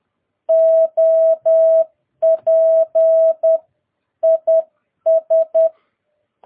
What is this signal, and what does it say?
Morse code, heard as a keyed tone, about 10 wpm, 650 Hz. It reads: OP IS
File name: jh8cbh / 2022cw